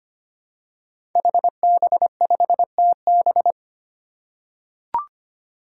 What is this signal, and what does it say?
Morse code, heard as a keyed tone, about 25 wpm, 700 Hz. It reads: HB5TB
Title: Five Challenging Letter-Number Combinations 25wpm